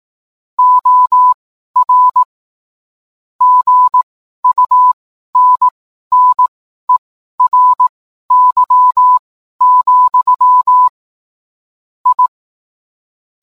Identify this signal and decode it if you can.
Morse code, heard as a keyed tone, about 18 wpm, spaced as for 12 wpm, 1000 Hz. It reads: OR GUNNERY, I
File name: cwblog / cw mp3 directory / mem morse 12